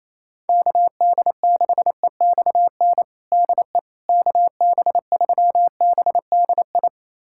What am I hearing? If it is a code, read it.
Morse code, heard as a keyed tone, about 28 wpm, 700 Hz. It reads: KD6EXN DE KB3BDI
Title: USA Callsign DE Callsign 28wpm